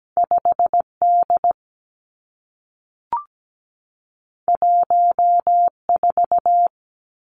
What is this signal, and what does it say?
Morse code, heard as a keyed tone, about 17 wpm, 700 Hz. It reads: 5 D 1 4
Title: Two Challenging Letter-Number Combinations 17wpm